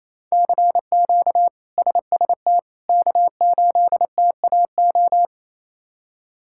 Morse code, heard as a keyed tone, about 27 wpm, 700 Hz. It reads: CQ SST K8TAO
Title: K1USN SST QSO Practice - 28wpm